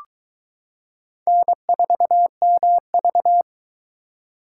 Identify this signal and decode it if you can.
Morse code, heard as a keyed tone, about 22 wpm, 700 Hz. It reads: N4MV